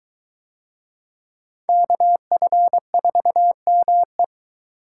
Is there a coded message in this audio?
Yes: Morse code from 1.7 to 4.2 s, heard as a keyed tone, about 21 wpm, 700 Hz. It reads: KF4ME